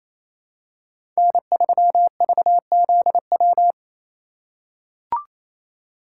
Morse code, heard as a keyed tone, about 28 wpm, 700 Hz. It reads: N3VZW
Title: USA Call Signs - 2x Character Spacing - 28wpm